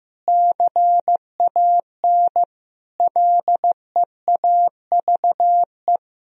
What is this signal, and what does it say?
Morse code, heard as a keyed tone, about 16 wpm, 700 Hz. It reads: CAN LEAVE
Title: Sentences from Top 1000 Words - Review Part 3 - 15wpm